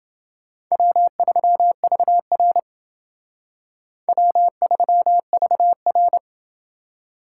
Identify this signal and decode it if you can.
Morse code, heard as a keyed tone, about 29 wpm, 700 Hz. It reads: W3VR W3VR